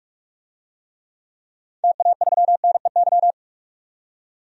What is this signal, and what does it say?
Morse code, heard as a keyed tone, about 47 wpm, 700 Hz. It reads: TA3NEY
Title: DXCC-100 Call Signs - 6 Characters - 45wpm